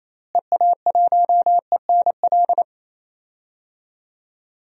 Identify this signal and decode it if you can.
Morse code, heard as a keyed tone, about 28 wpm, 700 Hz. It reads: EA1ENL
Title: DXCC-100 Call Signs - 6 Characters - Repeated 3x - 28wpm